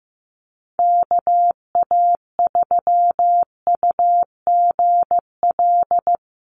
Morse code, heard as a keyed tone, about 15 wpm, 700 Hz. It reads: KA3UGL
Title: USA Call Signs Repeated 2x 15wpm